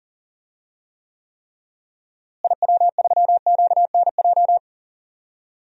Morse code, heard as a keyed tone, about 37 wpm, 700 Hz. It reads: IW3QNJ
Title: DXCC-100 Call Signs - 6 Characters - Repeated 3x - 40wpm